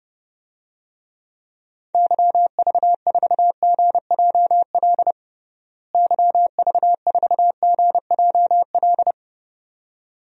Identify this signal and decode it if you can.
Morse code, heard as a keyed tone, about 32 wpm, 700 Hz. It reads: YV4GJL YV4GJL
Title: DXCC-100 Call Signs - 6 Characters - Repeated 2x - 30wpm